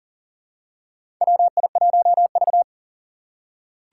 Morse code, heard as a keyed tone, about 40 wpm, 700 Hz. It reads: WI1V
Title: USA Call Signs - 4 Characters - 6x Character Spacing - 40wpm